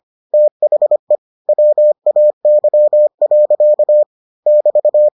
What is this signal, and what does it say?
Morse code, heard as a keyed tone, about 25 wpm, 600 Hz. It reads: THE WAY. <BT>